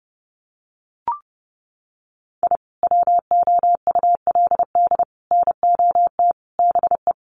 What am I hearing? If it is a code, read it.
Morse code, heard as a keyed tone, about 30 wpm, 700 Hz. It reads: I WOULD NOT BE
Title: Sets of 5 Words Encore2 30wpm